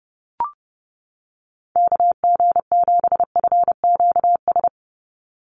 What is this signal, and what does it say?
Morse code, heard as a keyed tone, about 30 wpm, 700 Hz. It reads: KG7FQS
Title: USA Call Signs 30wpm